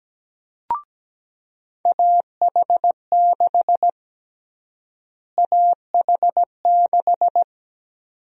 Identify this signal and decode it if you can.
Morse code, heard as a keyed tone, about 17 wpm, 700 Hz. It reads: AH6 AH6
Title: USA Call Sign Prefixes - Repeated 2x - 17wpm